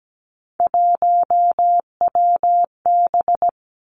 Morse code, heard as a keyed tone, about 17 wpm, 700 Hz. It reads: 1WB
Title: Three Challenging Letter-Number Combinations 17wpm